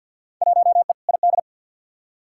Morse code, heard as a keyed tone, about 53 wpm, 700 Hz. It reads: 1E ID
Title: Field Day Exchanges - 50wpm